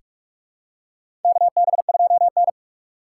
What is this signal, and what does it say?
Morse code, heard as a keyed tone, about 43 wpm, 700 Hz. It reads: KB2N
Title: USA Call Signs - 3x Character Spacing - 45wpm